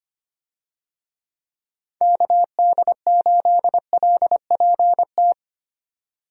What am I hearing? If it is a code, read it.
Morse code, heard as a keyed tone, about 25 wpm, 700 Hz. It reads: KD8LPT
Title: USA Call Signs - 6 Characters - 25wpm